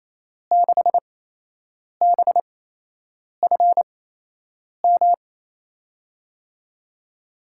Morse code, heard as a keyed tone, about 28 wpm, spaced as for 8 wpm, 700 Hz. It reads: 6BFM
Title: USA Call Signs - 6 Characters - 8x Character Spacing - 28wpm